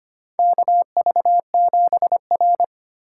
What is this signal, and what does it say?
Morse code, heard as a keyed tone, about 25 wpm, 700 Hz. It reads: KV7R
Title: USA Call Signs - 4 Characters - 4x Character Spacing - 25wpm